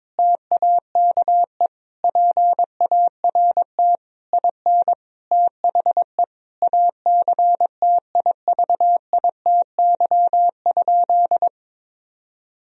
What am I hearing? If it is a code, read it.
Morse code, heard as a keyed tone, about 22 wpm, 700 Hz. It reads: TAKE PART IN THE ACTIVITY?
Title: Sentences from Top 1000 Words - Review Part 4 - 22wpm